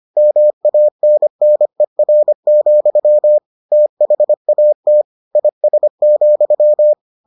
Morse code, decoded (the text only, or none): MANNER, THAT IS,